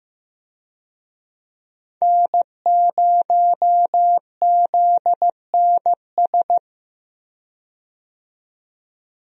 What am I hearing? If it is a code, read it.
Morse code, heard as a keyed tone, about 15 wpm, 700 Hz. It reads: N0ZNS
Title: USA Call Signs - 5 Characters - 15wpm